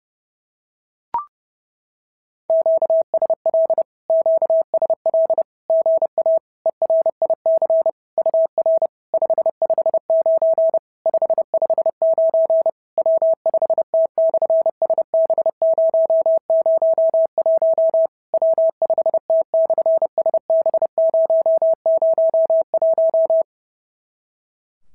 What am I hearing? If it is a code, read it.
Morse code, heard as a keyed tone, about 30 wpm, 650 Hz. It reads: QSL QSL GA ERIC UR 559 559 W5T/SB001 W5T/SB001
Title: SOTA QSO Practice 30wpm